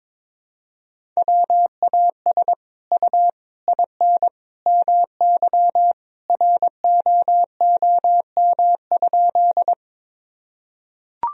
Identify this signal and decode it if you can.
Morse code, heard as a keyed tone, about 22 wpm, 700 Hz. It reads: WAS U IN MY ROOM?
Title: Sentences from Top 300 Words and Abbreviations 22wpm